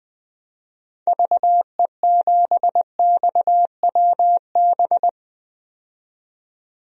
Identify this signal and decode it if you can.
Morse code, heard as a keyed tone, about 21 wpm, 700 Hz. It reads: VE7XWB
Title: DXCC-100 Call Signs - 6 Characters - 20wpm